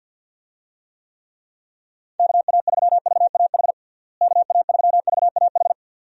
Morse code, heard as a keyed tone, about 50 wpm, 700 Hz. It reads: KA3VAH KA3VAH
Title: USA Call Signs - 6 Characters - Repeated 2x - 50wpm